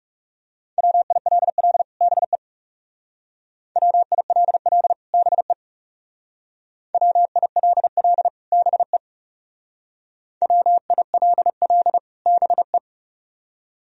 Morse code, heard as a keyed tone, about 35 wpm, 700 Hz. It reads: WILL BE WILL BE WILL BE WILL BE
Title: Speed-Racing - Sets of 2 Words 30wpm